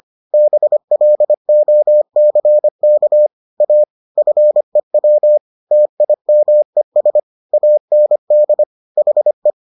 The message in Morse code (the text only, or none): BLOCK A FEW TIMES AND HE